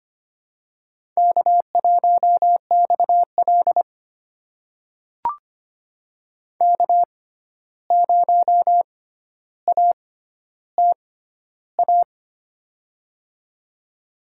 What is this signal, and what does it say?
Morse code, heard as a keyed tone, about 25 wpm, spaced as for 12 wpm, 700 Hz. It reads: K1XL K0ATA
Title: USA Call Signs - 6x Character Spacing - 25wpm